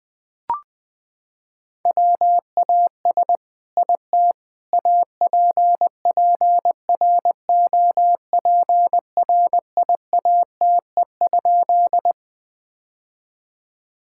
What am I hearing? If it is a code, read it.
Morse code, heard as a keyed tone, about 20 wpm, 700 Hz. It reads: WAS IT APPROPRIATE?
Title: Sentences from Top 1000 Words - Review Part 5 - 20wpm